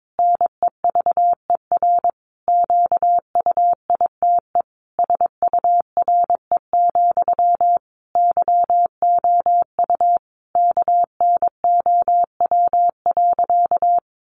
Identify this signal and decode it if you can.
Morse code, heard as a keyed tone, about 22 wpm, 700 Hz. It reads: NEVER QUITE SURE, YOU KNOW.
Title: Sentences from Top 700 Words 22wpm